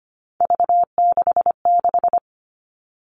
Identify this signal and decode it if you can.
Morse code, heard as a keyed tone, about 25 wpm, 700 Hz. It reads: V66
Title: All Cut Numbers - 3-Digit 25wpm